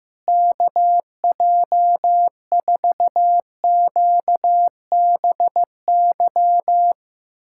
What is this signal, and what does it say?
Morse code, heard as a keyed tone, about 15 wpm, 700 Hz. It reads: KJ4QBY
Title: USA Callsign DE Callsign - 6 Characters - 15wpm